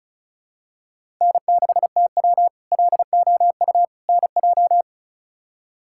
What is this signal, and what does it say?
Morse code, heard as a keyed tone, about 36 wpm, 700 Hz. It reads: N6TW LOU NJ
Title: K1USN SST QSO Practice - 35wpm